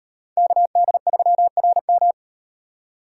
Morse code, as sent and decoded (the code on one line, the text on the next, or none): -.- -.. ...-- .-. --
KD3RM